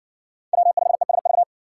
Morse code, heard as a keyed tone, about 60 wpm, 700 Hz. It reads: W5ESV